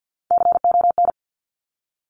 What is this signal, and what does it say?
Morse code, heard as a keyed tone, about 50 wpm, 700 Hz. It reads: CON
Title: Top Three Letter Combinations in Words 50wpm